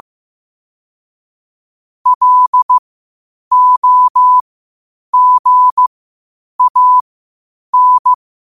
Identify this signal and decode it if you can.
Morse code, heard as a keyed tone, about 15 wpm, spaced as for 8 wpm, 1000 Hz. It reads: LOGAN
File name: qotdfiles / 8wpm